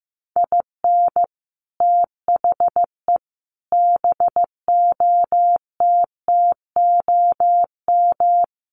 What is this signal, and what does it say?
Morse code, heard as a keyed tone, about 15 wpm, 700 Hz. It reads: IN THE BOTTOM
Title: Sets of 5 Words Encore 15wpm